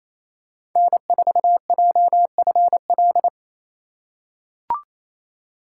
Morse code, heard as a keyed tone, about 28 wpm, 700 Hz. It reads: N4JFL
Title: USA Call Signs - 5 Characters - 2x Character Spacing - 28wpm